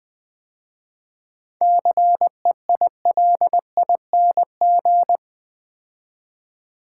Morse code, heard as a keyed tone, about 21 wpm, 690 Hz. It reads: CEILING
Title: 7 Letter Words - First 3 Letters Given - 20wpm